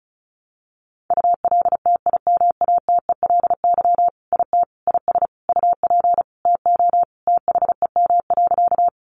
Morse code, heard as a keyed tone, about 34 wpm, 700 Hz. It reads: ULTIMATELY IT IS UP TO THEM.
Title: Sentences from Top 1200 Words 35wpm